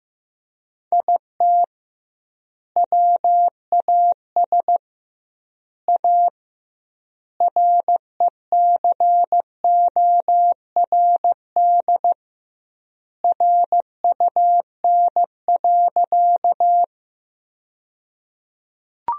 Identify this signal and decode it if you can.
Morse code, heard as a keyed tone, about 16 wpm, 700 Hz. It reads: IT WAS A RECORD RUN.
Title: 2x Word-Spacing - Sentences from Top 500 Words - 15wpm